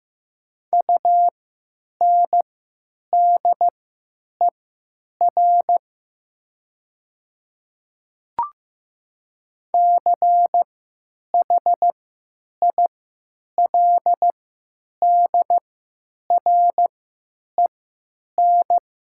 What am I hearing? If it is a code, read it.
Morse code, heard as a keyed tone, about 15 wpm, spaced as for 9 wpm, 700 Hz. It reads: UNDER CHILDREN
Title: 3x Character Spacing - Top 300 Words - 15wpm